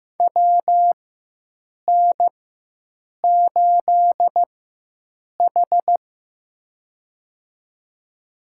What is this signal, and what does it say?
Morse code, heard as a keyed tone, about 15 wpm, spaced as for 7 wpm, 700 Hz. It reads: WN8H
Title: USA Call Signs - 4 Characters - 4x Character Spacing - 15wpm